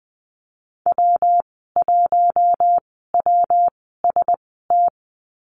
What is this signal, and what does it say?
Morse code, heard as a keyed tone, about 20 wpm, spaced as for 15 wpm, 700 Hz. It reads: W1WST
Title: 167 Lesson 27g - Number 7 - Call Signs 20wpm